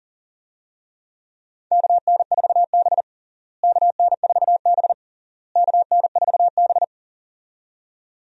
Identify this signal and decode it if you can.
Morse code, heard as a keyed tone, about 40 wpm, 700 Hz. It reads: KN4B KN4B KN4B
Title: USA Call Signs - 4 Characters - Repeated 3x - 40wpm